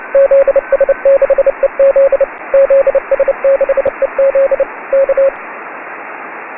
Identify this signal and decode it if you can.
Morse code, heard as a keyed tone, about 30 wpm, 550 Hz, with radio noise. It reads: ZS6EZ ZS6EZ K
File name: 2014-3-19-zs6ez-cw-cq